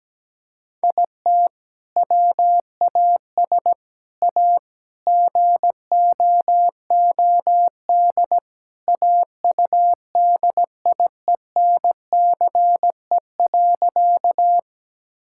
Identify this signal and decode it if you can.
Morse code, heard as a keyed tone, about 17 wpm, 700 Hz. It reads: IT WAS A GOOD AUDIENCE.
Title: Sentences from Top 1000 Words - Review Part 5 - 17wpm